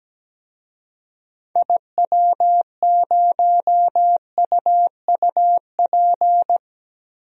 Morse code, heard as a keyed tone, about 17 wpm, 700 Hz. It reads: IW0UUP
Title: DXCC-100 Call Signs - 6 Characters - 17wpm